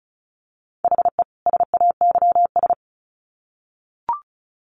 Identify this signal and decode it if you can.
Morse code, heard as a keyed tone, about 34 wpm, 700 Hz. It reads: HE SAYS
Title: Sets of 2 Words 35wpm